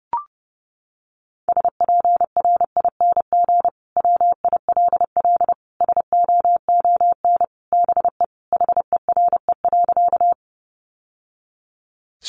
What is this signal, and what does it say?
Morse code, heard as a keyed tone, about 30 wpm, 700 Hz. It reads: SPRING WILL SOON BE HERE.
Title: Sentences from Top 1200 Words 30wpm